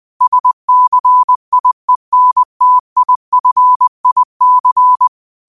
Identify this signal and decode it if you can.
Morse code, heard as a keyed tone, about 20 wpm, 1000 Hz. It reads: SCIENTIFIC
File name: qotdfiles / 20wpm